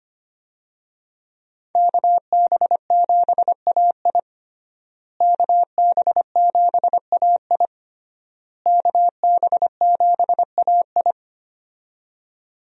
Morse code, heard as a keyed tone, about 25 wpm, 700 Hz. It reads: KB7AI KB7AI KB7AI
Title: USA Call Signs - 5 Characters - Repeated 3x - 25wpm